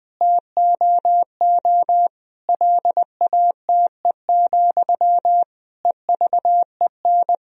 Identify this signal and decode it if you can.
Morse code, heard as a keyed tone, about 20 wpm, 700 Hz. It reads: TOO LATE, EVEN
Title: Sentences from Top 1000 Words - Review Part 1 - 20wpm